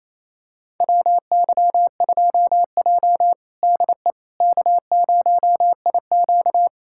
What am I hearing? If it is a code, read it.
Morse code, heard as a keyed tone, about 28 wpm, 700 Hz. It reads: WY2J DE K0IQ
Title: USA Callsign DE Callsign - 4 Characters - 28wpm